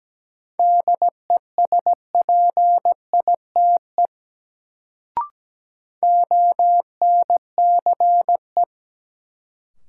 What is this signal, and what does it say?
Morse code, heard as a keyed tone, about 17 wpm, 700 Hz. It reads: DESPITE ONCE
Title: Sets of 4 Words - New Word Drill - 17wpm